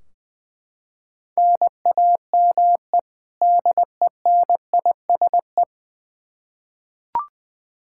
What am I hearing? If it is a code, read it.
Morse code, heard as a keyed tone, about 20 wpm, 700 Hz. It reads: NAME DENISE
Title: Practice QSO - Exchange 1 - Beginning - 20wpm